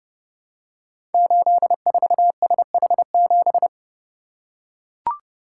Morse code, heard as a keyed tone, about 30 wpm, 700 Hz. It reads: 84SH7
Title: Code Groups - 5 Letter:Numbers - 30wpm